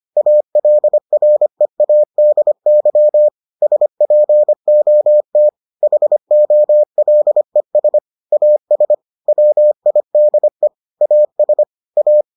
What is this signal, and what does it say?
Morse code, heard as a keyed tone, about 25 wpm, 600 Hz. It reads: ALREADY SPOT HOLES AS WIDE AS A